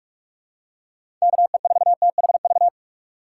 Morse code, heard as a keyed tone, about 46 wpm, 700 Hz. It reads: KE4THV
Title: USA Call Signs - 6 Characters - 4x Character Spacing - 45wpm